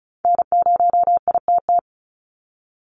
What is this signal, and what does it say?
Morse code, heard as a keyed tone, about 34 wpm, 700 Hz. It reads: N0ITT
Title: USA Call Signs 35wpm